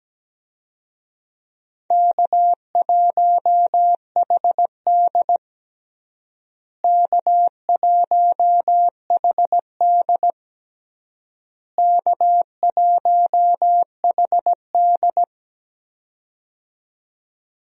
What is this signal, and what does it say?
Morse code, heard as a keyed tone, about 17 wpm, 700 Hz. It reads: K1HD K1HD K1HD
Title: USA Call Signs - 4 Characters - Repeated 3x - 17wpm